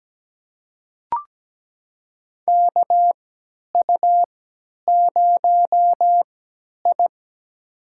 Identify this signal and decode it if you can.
Morse code, heard as a keyed tone, about 17 wpm, spaced as for 10 wpm, 700 Hz. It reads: KU0I
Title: USA Call Signs - 4 Characters - 3x Character Spacing - 17wpm